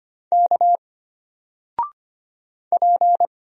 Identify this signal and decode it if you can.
Morse code, heard as a keyed tone, about 25 wpm, 700 Hz. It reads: K P
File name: Single-Letter-Number-25wpm